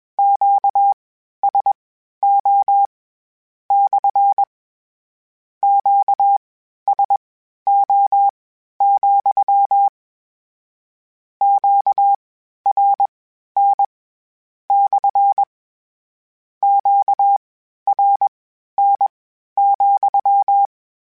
Morse code, heard as a keyed tone, about 21 wpm, spaced as for 12 wpm, 800 Hz. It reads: QSO / QSO, QRN / QRN,
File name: Q Calls / Q Calls-mp3s / Q Calls SPSL2 @20